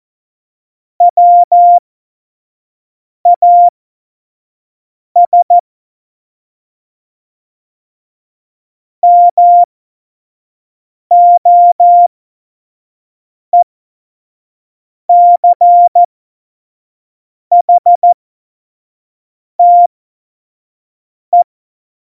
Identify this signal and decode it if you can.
Morse code, heard as a keyed tone, about 14 wpm, spaced as for 5 wpm, 700 Hz. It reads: WAS MOECHTE
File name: Morsekurs / Witze / Witz 21 April 2020, 25 F70